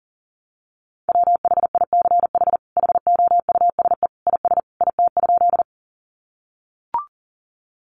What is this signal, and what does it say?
Morse code, heard as a keyed tone, about 40 wpm, 700 Hz. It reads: WHICH HOUSE IS IT?
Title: Sentences from Top 200 Words 40wpm